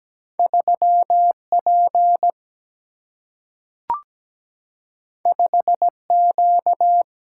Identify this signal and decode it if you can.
Morse code, heard as a keyed tone, about 17 wpm, 700 Hz. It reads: 3P 5Q
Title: Single-Letter Single-Number 17wpm